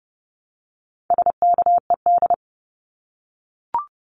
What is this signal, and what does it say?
Morse code, heard as a keyed tone, about 30 wpm, 700 Hz. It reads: SKED
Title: Abbreviations 30wpm